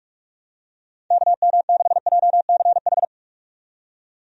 Morse code, heard as a keyed tone, about 41 wpm, 700 Hz. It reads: KM6JCH
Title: USA Call Signs - 6 Characters - 45wpm